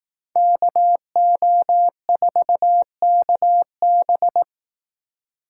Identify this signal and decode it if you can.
Morse code, heard as a keyed tone, about 18 wpm, 700 Hz. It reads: KO4KB
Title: Speed Racing - USA 2x2 Call Signs 15wpm